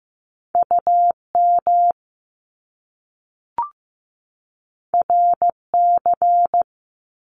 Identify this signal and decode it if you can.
Morse code, heard as a keyed tone, about 15 wpm, 700 Hz. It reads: UM RC
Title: Top Two Letter Combinations in Words 15wpm